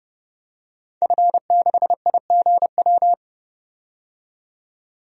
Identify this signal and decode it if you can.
Morse code, heard as a keyed tone, about 30 wpm, 700 Hz. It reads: F6IGW
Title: DXCC-100 Call Signs - 5 Characters - 30wpm